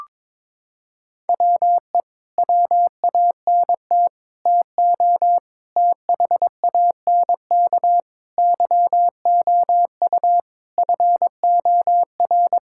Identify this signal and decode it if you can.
Morse code, heard as a keyed tone, about 22 wpm, 700 Hz. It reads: WE WANT TO THANK YOU FOR
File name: Sets of 9 Words - 22wpm